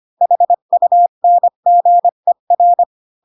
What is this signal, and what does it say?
Morse code, heard as a keyed tone, about 25 wpm, 700 Hz. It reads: HUNGER
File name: Pinocchio / Pinocchio ch8 20 25 ik1hge